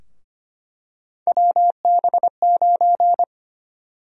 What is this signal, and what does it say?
Morse code, heard as a keyed tone, about 25 wpm, 700 Hz. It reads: WB9
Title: USA Call Sign Prefixes - Repeated 2x - 25wpm